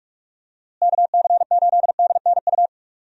Morse code, heard as a keyed tone, about 46 wpm, 700 Hz. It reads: KC8DNU
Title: USA Call Signs - 6 Characters - Repeated 3x - 45wpm